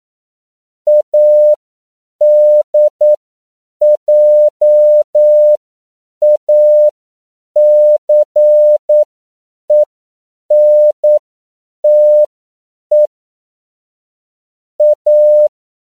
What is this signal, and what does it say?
Morse code, heard as a keyed tone, about 9 wpm, spaced as for 7 wpm, 600 Hz. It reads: ADJACENTE A
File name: CW / 10PPM